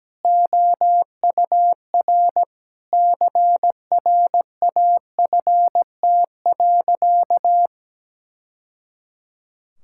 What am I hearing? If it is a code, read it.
Morse code, heard as a keyed tone, about 17 wpm, 700 Hz. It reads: OUR CRAFT.
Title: Sentences from Top 1000 Words - Review Part 6 - 17wpm